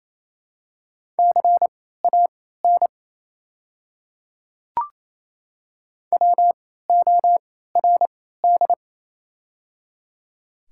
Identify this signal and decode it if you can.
Morse code, heard as a keyed tone, about 27 wpm, spaced as for 16 wpm, 700 Hz. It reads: CAN WORD
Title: 3x Character Spacing - Top 100 Words - 28wpm